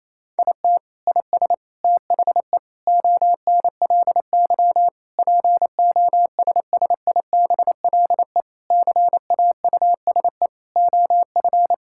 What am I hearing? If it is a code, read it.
Morse code, heard as a keyed tone, about 28 wpm, 700 Hz. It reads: IT IS THE ONLY POSSIBLE CAUSE OF
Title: Sentences from Top 1600 Words 28wpm